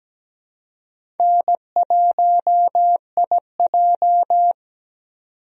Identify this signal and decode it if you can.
Morse code, heard as a keyed tone, about 17 wpm, 700 Hz. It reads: N1IJ
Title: USA Call Signs - 4 Characters - 17wpm